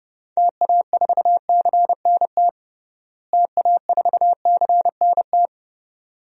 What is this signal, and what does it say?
Morse code, heard as a keyed tone, about 30 wpm, 700 Hz. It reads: TA4CNT TA4CNT
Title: DXCC-100 Call Signs - 6 Characters - Repeated 2x - 30wpm